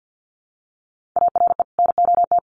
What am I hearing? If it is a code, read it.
Morse code, heard as a keyed tone, about 51 wpm, 700 Hz. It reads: ARE NOT
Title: Sets of 2 Words 50wpm